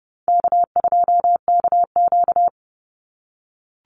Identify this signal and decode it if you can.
Morse code, heard as a keyed tone, about 30 wpm, 700 Hz. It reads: K2KQ
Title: Easy Call Signs - 30wpm